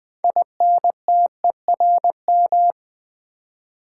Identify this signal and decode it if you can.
Morse code, heard as a keyed tone, about 20 wpm, 700 Hz. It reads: INTERM
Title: Build A Word - 6 Letters At A Time - 12-Letter Words - 20wpm